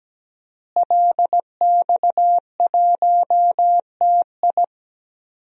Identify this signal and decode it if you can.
Morse code, heard as a keyed tone, about 17 wpm, 700 Hz. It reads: LX1TI
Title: DXCC-100 Call Signs - 5 Characters - Repeated 3x - 17wpm